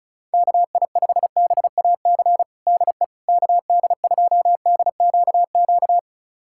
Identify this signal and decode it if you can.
Morse code, heard as a keyed tone, about 36 wpm, 700 Hz. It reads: KI5BAC DE KD2DQQ
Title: USA Callsign DE Callsign - 6 Characters - 35wpm